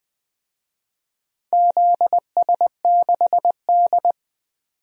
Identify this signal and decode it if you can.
Morse code, heard as a keyed tone, about 20 wpm, 700 Hz. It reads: ZS6D
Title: DXCC-100 Call Signs - 4 Characters - 20wpm